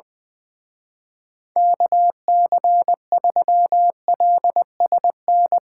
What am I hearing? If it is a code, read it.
Morse code, heard as a keyed tone, about 21 wpm, 700 Hz. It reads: KC3LSN
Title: Speed Racing - USA 2x3 Call Signs 20wpm